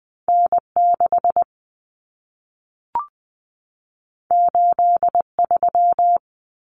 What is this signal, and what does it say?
Morse code, heard as a keyed tone, about 20 wpm, 700 Hz. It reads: N6 83
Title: Common Cut Numbers - 2-Digit 20wpm